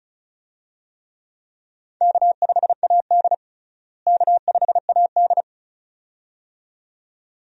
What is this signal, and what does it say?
Morse code, heard as a keyed tone, about 34 wpm, 700 Hz. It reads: K5AD K5AD
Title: USA Call Signs - 4 Characters - Repeated 2x - 35wpm